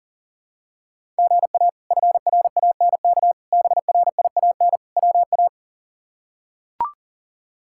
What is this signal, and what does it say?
Morse code, heard as a keyed tone, about 40 wpm, 700 Hz. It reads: GA FRANK BRIAN WA
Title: K1USN SST QSO Practice - 40wpm